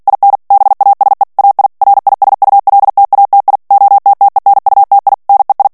Morse code, heard as a keyed tone, about 48 wpm, 800 Hz. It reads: IN BASE AI RISULTATI OTTENUTI NEI
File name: ITALIEN texte clair 2 vitesse 45 mots minute